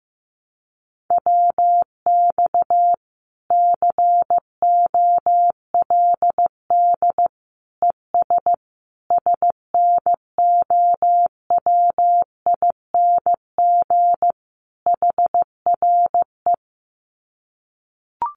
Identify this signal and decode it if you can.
Morse code, heard as a keyed tone, about 15 wpm, 700 Hz. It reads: WX COLD ES SNOWING HRE
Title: Common QSO Phrases 15wpm